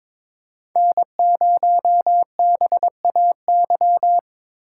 Morse code, heard as a keyed tone, about 22 wpm, 700 Hz. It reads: N0BAY
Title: USA Call Signs - 3x Character Spacing - 22wpm